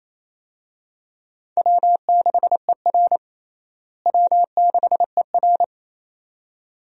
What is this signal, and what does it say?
Morse code, heard as a keyed tone, about 27 wpm, 700 Hz. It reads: W6ER W6ER
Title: USA Call Signs - 4 Characters - Repeated 2x - 28wpm